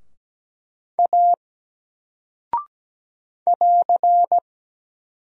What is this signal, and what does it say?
Morse code, heard as a keyed tone, about 17 wpm, 700 Hz. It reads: A <AR>
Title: Single Character (Letter:Number:Punctuation:Prosign) 17wpm